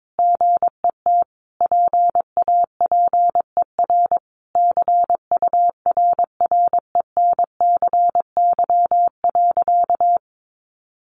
Morse code, heard as a keyed tone, about 22 wpm, 700 Hz. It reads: GET PAPER CURRENCY.
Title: Sentences from Top 1500 Words 22wpm